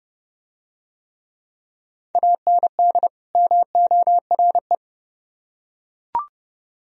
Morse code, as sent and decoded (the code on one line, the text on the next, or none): .- -. -.. / -- --- .-. .
AND MORE